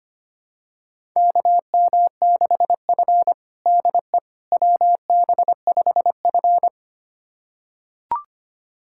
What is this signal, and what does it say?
Morse code, heard as a keyed tone, about 25 wpm, 700 Hz. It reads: KM6F DE WB5F
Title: USA Callsign DE Callsign - 4 Characters - 25wpm